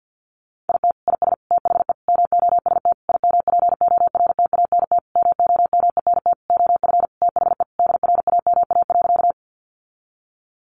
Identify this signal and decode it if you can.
Morse code, heard as a keyed tone, about 50 wpm, 700 Hz. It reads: IT IS THE MOST IMPORTANT MOMENT OF THE DRAMA.
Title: Sentences from Top 1500 Words 50wpm